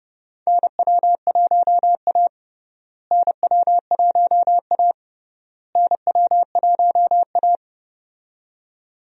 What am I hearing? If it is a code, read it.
Morse code, heard as a keyed tone, about 32 wpm, 700 Hz. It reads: NW1A NW1A NW1A
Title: USA Call Signs - 4 Characters - Repeated 3x - 30wpm